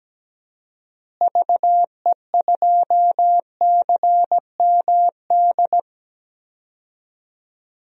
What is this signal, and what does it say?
Morse code, heard as a keyed tone, about 17 wpm, 700 Hz. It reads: VE2CMD
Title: DXCC-100 Call Signs - 6 Characters - Repeated 3x - 17wpm